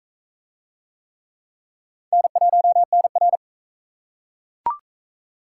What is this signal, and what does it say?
Morse code, heard as a keyed tone, about 37 wpm, 700 Hz. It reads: N1NR